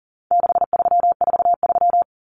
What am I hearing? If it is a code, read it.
Morse code, heard as a keyed tone, about 40 wpm, 700 Hz. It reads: 6343
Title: Common Cut Numbers - 4-Digit 40wpm